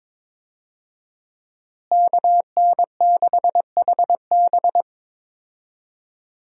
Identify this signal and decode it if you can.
Morse code, heard as a keyed tone, about 22 wpm, 700 Hz. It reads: KN6HB
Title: USA Call Signs - 5 Characters - Repeated 3x - 22wpm